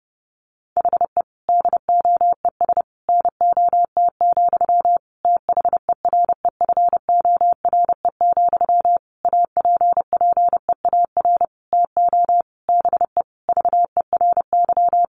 Morse code, heard as a keyed tone, about 28 wpm, 700 Hz. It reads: HE DOES NOT, THEREFORE, APPEAR TO BE VERY
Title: Sentences from Top 1200 Words 30wpm